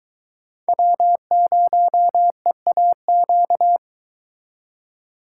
Morse code, heard as a keyed tone, about 23 wpm, 700 Hz. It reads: W0EAQ